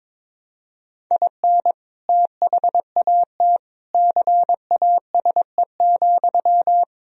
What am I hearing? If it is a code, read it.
Morse code, heard as a keyed tone, about 22 wpm, 700 Hz. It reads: IN THAT CASE,
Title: Sentences from Top 1000 Words - Review Part 1 - 22wpm